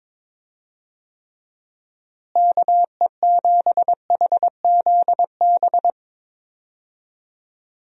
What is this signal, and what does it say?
Morse code, heard as a keyed tone, about 22 wpm, 700 Hz. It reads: KE7HZB
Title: USA Call Signs - 6 Characters - Repeated 2x - 22wpm